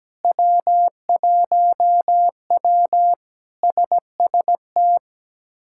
Morse code, heard as a keyed tone, about 17 wpm, 700 Hz. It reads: W1W SST
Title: K1USN SST QSO Practice - 17wpm